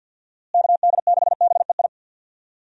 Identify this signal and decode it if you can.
Morse code, heard as a keyed tone, about 51 wpm, 700 Hz. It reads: KD6BEI
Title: USA Call Signs - 6 Characters - Repeated 3x - 50wpm